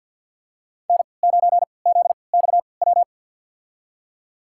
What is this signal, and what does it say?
Morse code, heard as a keyed tone, about 51 wpm, spaced as for 28 wpm, 700 Hz. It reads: N9ZXW
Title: USA Call Signs - 3x Character Spacing - 50wpm